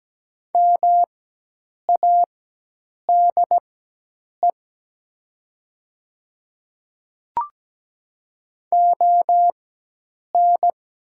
Morse code, heard as a keyed tone, about 17 wpm, spaced as for 8 wpm, 700 Hz. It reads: MADE ON
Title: 4x Character Spacing - Top 300 Words - 17wpm